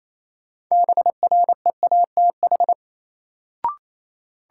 Morse code, heard as a keyed tone, about 28 wpm, 700 Hz. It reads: BREATH
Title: Sets of 5 Words Encore - New Word Drill - 28wpm